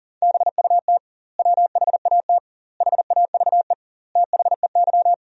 Morse code, heard as a keyed tone, about 40 wpm, 700 Hz. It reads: BUT WHAT HAVE THEY
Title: 2x Word-Spacing - Sentences from Top 100 Words - 40wpm